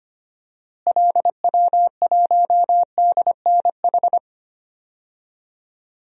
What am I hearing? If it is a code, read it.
Morse code, heard as a keyed tone, about 25 wpm, 700 Hz. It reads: LW1DNH
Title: DXCC-100 Call Signs - 6 Characters - Repeated 2x - 25wpm